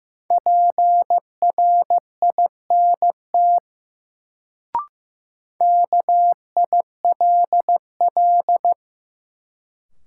Morse code, heard as a keyed tone, about 15 wpm, 700 Hz. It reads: PRINT KILL
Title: Sentences from Top 1400 Words - New Word Drill Alt - 15wpm